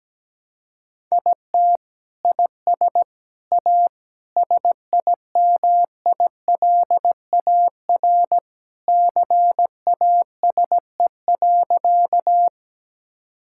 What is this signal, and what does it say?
Morse code, heard as a keyed tone, about 17 wpm, 700 Hz. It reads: IT IS A SIMILAR CASE.
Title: Sentences from Top 1000 Words - Review Part 5 - 17wpm